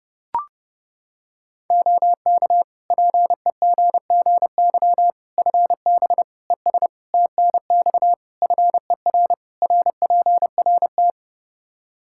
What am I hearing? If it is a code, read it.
Morse code, heard as a keyed tone, about 32 wpm, 700 Hz. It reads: OK PEGGY FB ES TNX FER RPRT
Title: Practice QSO - Exchange 2 - Info - 30wpm